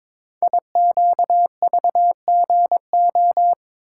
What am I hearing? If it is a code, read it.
Morse code, heard as a keyed tone, about 22 wpm, 700 Hz. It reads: IQVGO
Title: Code Groups - 5 Letters - 22wpm